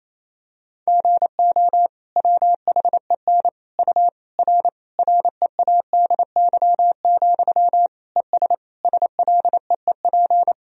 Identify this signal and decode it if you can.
Morse code, heard as a keyed tone, about 28 wpm, 700 Hz. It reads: GO WHEN U R READY, ES SLEEP
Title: Sentences from Top 500 Words and Abbreviations 28wpm